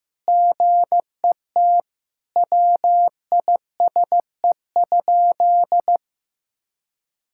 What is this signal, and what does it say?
Morse code, heard as a keyed tone, about 15 wpm, 700 Hz. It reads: GET WISE?
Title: Sentences from Top 1600 Words 15wpm